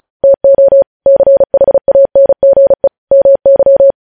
Morse code, heard as a keyed tone, about 36 wpm, 550 Hz. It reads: TO CHANGE MY
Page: News headlines in Morse code at 35 wpm